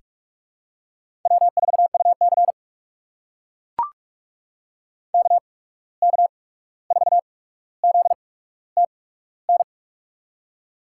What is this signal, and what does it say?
Morse code, heard as a keyed tone, about 44 wpm, spaced as for 17 wpm, 700 Hz. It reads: W4UC KK4ZTN
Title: USA Call Signs - 8x Character Spacing - 45wpm